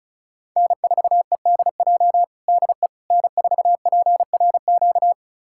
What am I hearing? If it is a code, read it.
Morse code, heard as a keyed tone, about 34 wpm, 700 Hz. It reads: N4EDJ DE N4PRQ
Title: USA Callsign DE Callsign - 5 Characters - 35wpm